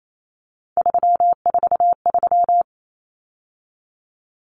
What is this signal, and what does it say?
Morse code, heard as a keyed tone, about 28 wpm, 700 Hz. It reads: 343
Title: Common Cut Numbers - 3-Digit 28wpm